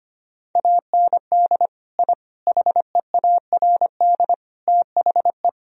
Morse code, heard as a keyed tone, about 25 wpm, 700 Hz. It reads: AND I HEARD THE
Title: Sentences from Top 1000 Words - Review Part 1 - 25wpm